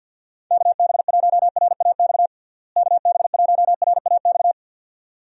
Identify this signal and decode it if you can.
Morse code, heard as a keyed tone, about 46 wpm, 700 Hz. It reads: KB1RAX KB1RAX
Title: USA Call Signs - 6 Characters - Repeated 2x - 50wpm